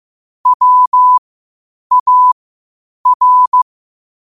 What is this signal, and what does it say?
Morse code, heard as a keyed tone, about 15 wpm, spaced as for 8 wpm, 1000 Hz. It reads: WAR
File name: qotdfiles / 8wpm